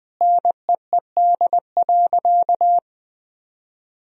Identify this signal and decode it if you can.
Morse code, heard as a keyed tone, about 21 wpm, 700 Hz. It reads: NEED.
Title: Sentences from Top 1000 Words - Review Part 4 - 20wpm